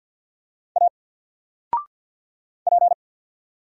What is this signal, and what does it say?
Morse code, heard as a keyed tone, about 50 wpm, 700 Hz. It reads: A P
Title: Single Letter-Number 50wpm